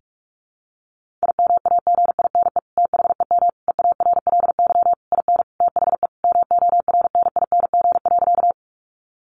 Morse code, heard as a keyed tone, about 44 wpm, 700 Hz. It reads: IMAGINE THEM EARLY IN THE MORNING.
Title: Sentences from Top 1400 Words 45wpm